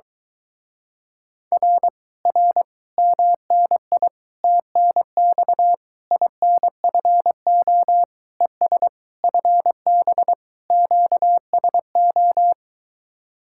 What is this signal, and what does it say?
Morse code, heard as a keyed tone, about 23 wpm, 700 Hz. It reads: R R MNI TNX INFO ES FB QSO